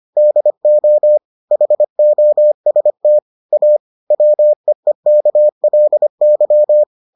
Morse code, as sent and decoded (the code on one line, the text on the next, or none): -.. --- / .... --- ... - / .- / .-- . -.- .-.. -.--
DO HOST A WEEKLY